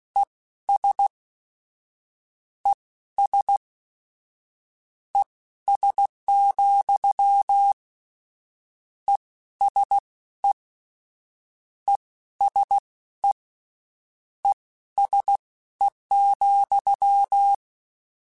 Morse code, heard as a keyed tone, about 16 wpm, spaced as for 12 wpm, 800 Hz. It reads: ES ES ES, ESE ESE ESE,
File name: General / General-mp3s / Suffixes sp @15